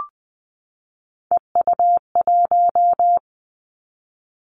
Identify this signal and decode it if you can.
Morse code, heard as a keyed tone, about 20 wpm, 700 Hz. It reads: EU1
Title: Callsign Prefixes for Top 50 DXCC - 20wpm